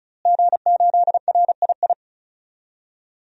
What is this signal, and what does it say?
Morse code, heard as a keyed tone, about 36 wpm, 700 Hz. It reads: G8RII
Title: DXCC-100 Call Signs - 5 Characters - Repeated 3x - 35wpm